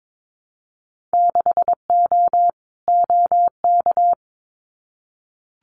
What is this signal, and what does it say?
Morse code, heard as a keyed tone, about 22 wpm, 700 Hz. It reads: 6O OK
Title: Winter Field Day Exchanges Encore 22wpm